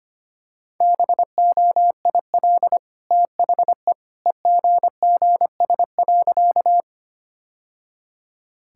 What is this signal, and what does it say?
Morse code, heard as a keyed tone, about 25 wpm, 700 Hz. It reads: BOIL THE EGGS.
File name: Easy Sentences with Hints - 25wpm